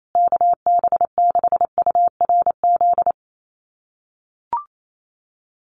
Morse code, heard as a keyed tone, about 28 wpm, 700 Hz. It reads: KB6URZ
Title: USA Call Signs Repeated 3x 28wpm